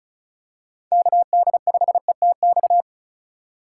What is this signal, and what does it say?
Morse code, heard as a keyed tone, about 34 wpm, 690 Hz. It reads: KD5ETX
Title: USA Call Signs - 6 Characters - 2x Character Spacing - 35wpm